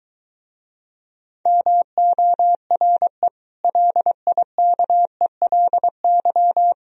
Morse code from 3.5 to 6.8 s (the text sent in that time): LIKELY